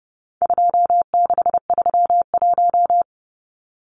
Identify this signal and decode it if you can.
Morse code, heard as a keyed tone, about 30 wpm, 700 Hz. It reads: 2631
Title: Numbers - 4-Digit 30wpm